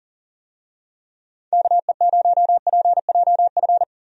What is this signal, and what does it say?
Morse code, heard as a keyed tone, about 40 wpm, 700 Hz. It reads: KE0PJF
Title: USA Call Signs - 6 Characters - Repeated 3x - 40wpm